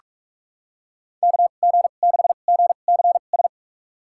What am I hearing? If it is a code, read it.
Morse code, heard as a keyed tone, about 46 wpm, spaced as for 32 wpm, 700 Hz. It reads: KG6GCS
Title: USA Call Signs - 6 Characters - 2x Character Spacing - 45wpm